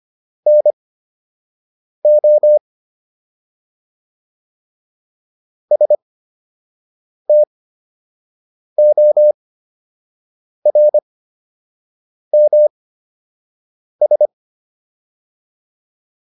Morse code, heard as a keyed tone, about 25 wpm, spaced as for 6 wpm, 600 Hz. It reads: NO STORMS